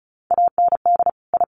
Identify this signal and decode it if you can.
Morse code, heard as a keyed tone, about 34 wpm, 700 Hz. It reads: AND I
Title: Sets of 5 Words Encore2 35wpm